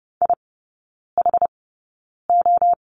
Morse code, heard as a keyed tone, about 30 wpm, spaced as for 9 wpm, 700 Hz. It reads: IHO
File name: Lesson 18e - Letter m - Three Characters 30wpm